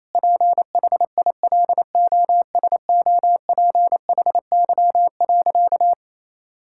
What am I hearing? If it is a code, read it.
Morse code, heard as a keyed tone, about 27 wpm, 700 Hz. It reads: PHILOSOPHY.
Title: Sentences from Top 1000 Words - Review Part 6 - 28wpm